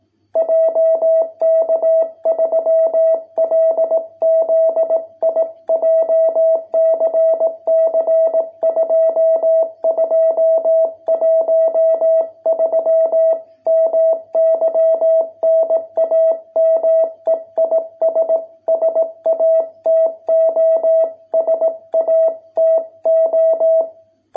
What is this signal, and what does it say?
Morse code, heard as a keyed tone, about 18 wpm, 650 Hz. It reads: JK3LZI JCC 2213 MY NAME IS SATO SATO